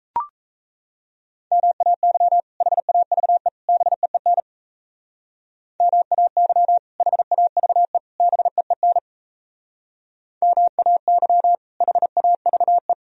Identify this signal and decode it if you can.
Morse code, heard as a keyed tone, about 36 wpm, 700 Hz. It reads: MAY HAVE BEEN MAY HAVE BEEN MAY HAVE